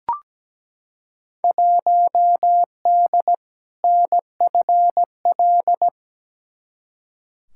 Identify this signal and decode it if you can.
Morse code, heard as a keyed tone, about 17 wpm, 700 Hz. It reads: 1D NFL
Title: Field Day Exchanges - 17wpm